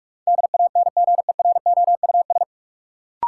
Morse code, heard as a keyed tone, about 46 wpm, 700 Hz. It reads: DANGEROUS
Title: Sentences from Top 1200 Words - New Word Drill - 45wpm